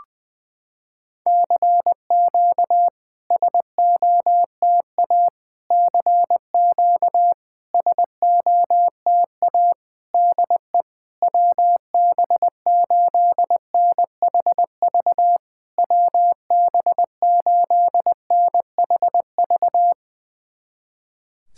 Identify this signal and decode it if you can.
Morse code, heard as a keyed tone, about 20 wpm, 700 Hz. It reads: CQ SOTA CQ SOTA DE WB8NHV WB8NHV
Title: SOTA QSO Practice 20wpm